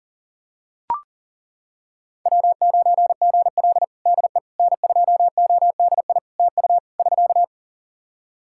Morse code, heard as a keyed tone, about 40 wpm, 700 Hz. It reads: W9GL DE N2ODI TU <SK>
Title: Practice QSO - Exchange 4 - Ending - 40wpm